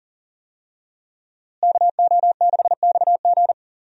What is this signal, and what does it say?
Morse code, heard as a keyed tone, about 40 wpm, 700 Hz. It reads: KO6XG
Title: USA Call Signs - 5 Characters - Repeated 3x - 40wpm